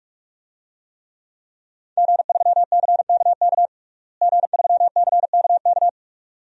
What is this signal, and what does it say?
Morse code, heard as a keyed tone, about 46 wpm, 700 Hz. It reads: G3CKK G3CKK
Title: DXCC-100 Call Signs - 5 Characters - Repeated 2x - 45wpm